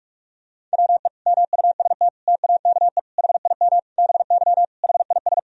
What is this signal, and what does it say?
Morse code, heard as a keyed tone, about 46 wpm, 700 Hz. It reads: WE MUST TAKE HIM BY HIS
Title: Sentences from Top 1000 Words - Review Part 1 - 45wpm